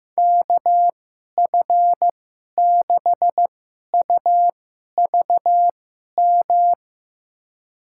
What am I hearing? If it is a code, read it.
Morse code, heard as a keyed tone, about 15 wpm, spaced as for 11 wpm, 700 Hz. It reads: KF6UVM
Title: USA Call Signs - 6 Characters - 2x Character Spacing - 15wpm